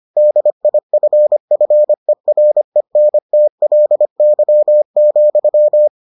Morse code, heard as a keyed tone, about 25 wpm, 600 Hz. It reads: DIFFERENTLY,